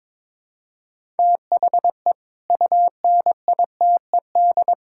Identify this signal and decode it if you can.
Morse code, heard as a keyed tone, about 22 wpm, 700 Hz. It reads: THE UNITED
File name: practice / 22wpm / Build A Phrase - 5-Word Phrases - 22wpm